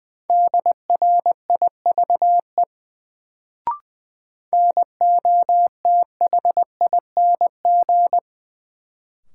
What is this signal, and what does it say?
Morse code, heard as a keyed tone, about 20 wpm, 700 Hz. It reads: DRIVE NOTHING
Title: Sentences from Top 500 Words - New Word Drill - 20wpm